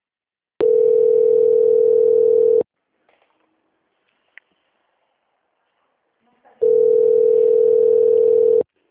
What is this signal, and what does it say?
Ringback tone (North America), 440 + 480 Hz.
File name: out- -2004- - - .333